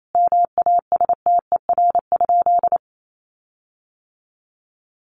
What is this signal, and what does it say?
Morse code, heard as a keyed tone, about 28 wpm, 700 Hz. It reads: MASTER?
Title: Sentences from Top 1300 Words 28wpm